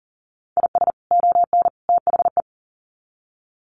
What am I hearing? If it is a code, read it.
Morse code, heard as a keyed tone, about 40 wpm, 700 Hz. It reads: IS ON THE
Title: Sets of 3 Words 40wpm